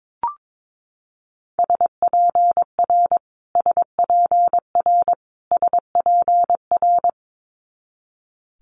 Morse code, heard as a keyed tone, about 22 wpm, 700 Hz. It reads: SPR SPR SPR
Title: Multi-letter Phonemes Repeated 3x 22wpm